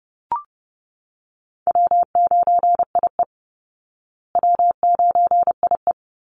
Morse code, heard as a keyed tone, about 30 wpm, 700 Hz. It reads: W9IE W9IE
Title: Call Signs Repeated 2x - 30wpm